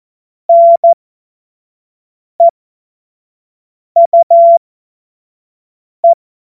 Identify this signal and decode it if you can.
Morse code, heard as a keyed tone, about 14 wpm, spaced as for 5 wpm, 680 Hz. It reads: NEUE